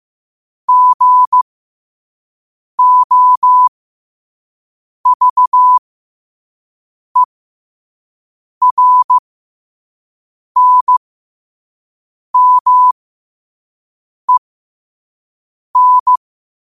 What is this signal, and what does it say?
Morse code, heard as a keyed tone, about 15 wpm, spaced as for 5 wpm, 1000 Hz. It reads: GOVERNMEN
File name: qotdfiles / 5wpm